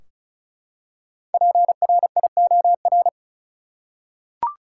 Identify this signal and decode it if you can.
Morse code, heard as a keyed tone, about 34 wpm, 700 Hz. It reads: PRIOR
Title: Five Letter Words - 35wpm